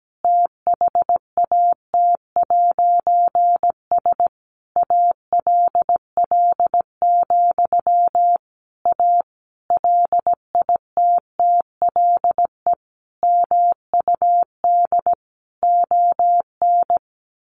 Morse code, heard as a keyed tone, about 17 wpm, 700 Hz. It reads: THAT'S ALL, A LITTLE MUD ON
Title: A Sound Of Thunder 17wpm